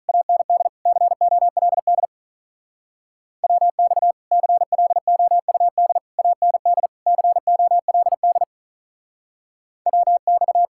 Morse code, heard as a keyed tone, about 44 wpm, 700 Hz. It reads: AND COLD WX CLOUD AND COLD WX